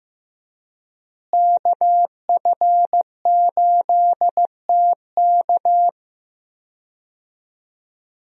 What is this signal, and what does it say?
Morse code, heard as a keyed tone, about 15 wpm, 700 Hz. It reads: KF8TK
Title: USA Call Signs - 5 Characters - Repeated 3x - 15wpm